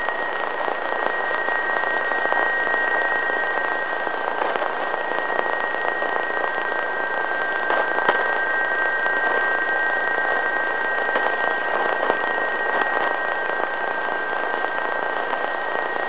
PSK31, about 31.25 baud, with radio noise.